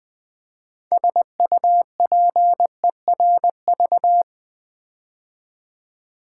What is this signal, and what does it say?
Morse code, heard as a keyed tone, about 20 wpm, 700 Hz. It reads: SUPERV
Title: Build A Word - 3 Letters At A Time - 9-Letter Words - 20wpm